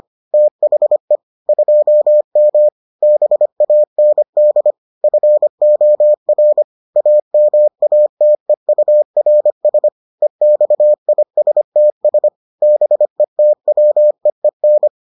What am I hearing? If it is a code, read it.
Morse code, heard as a keyed tone, about 25 wpm, 600 Hz. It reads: THE 2M BAND FOR AMATEURS EXISTS BETWEEN